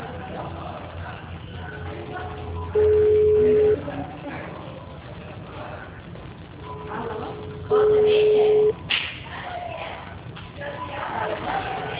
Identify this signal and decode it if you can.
Ringback tone (Europe, 425 Hz), with speech.